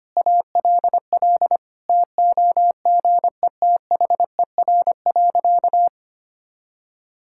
Morse code, heard as a keyed tone, about 25 wpm, 700 Hz. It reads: ALL TOGETHER.
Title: Sentences from Top 1000 Words - Review Part 6 - 25wpm